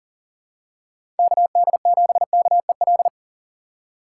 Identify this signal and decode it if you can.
Morse code, heard as a keyed tone, about 40 wpm, 700 Hz. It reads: KD7KEL
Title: USA Call Signs - 6 Characters - 2x Character Spacing - 40wpm